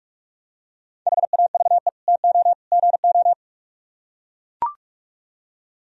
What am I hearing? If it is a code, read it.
Morse code, heard as a keyed tone, about 42 wpm, 700 Hz. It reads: HAVE TO GO